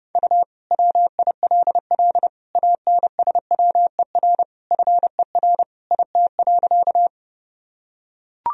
Morse code, heard as a keyed tone, about 30 wpm, 700 Hz. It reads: U WILL ANSWER FER IT.
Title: Sentences from Top 300 Words and Abbreviations 30wpm